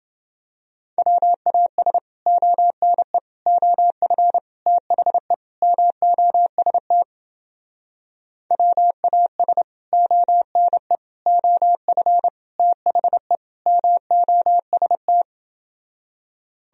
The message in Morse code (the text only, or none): WAS ONE OF THE MOST WAS ONE OF THE MOST